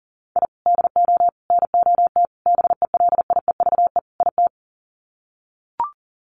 Morse code, heard as a keyed tone, about 40 wpm, 700 Hz. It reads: I DO NOT BELIEVE IT
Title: Sets of 5 Words 40wpm